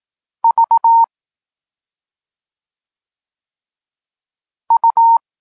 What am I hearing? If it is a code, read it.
Morse code, heard as a keyed tone, about 18 wpm, 920 Hz. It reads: V U